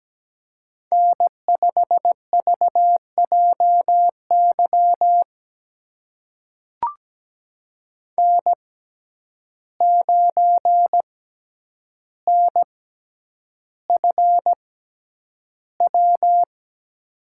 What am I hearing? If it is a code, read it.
Morse code, heard as a keyed tone, about 17 wpm, spaced as for 9 wpm, 700 Hz. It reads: N5VJY N9NFW
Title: USA Call Signs - 5 Characters - 6x Character Spacing - 17wpm